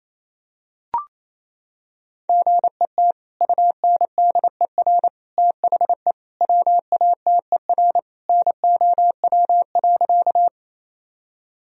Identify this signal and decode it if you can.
Morse code, heard as a keyed tone, about 28 wpm, 700 Hz. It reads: GET UNDER THE WATER NOW.
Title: Sentences from Top 1000 Words - Review Part 1 - 28wpm